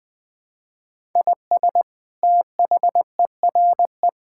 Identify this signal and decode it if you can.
Morse code, heard as a keyed tone, about 21 wpm, 700 Hz. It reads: IS THERE
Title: Sentences from Top 1000 Words - Review Part 6 - 20wpm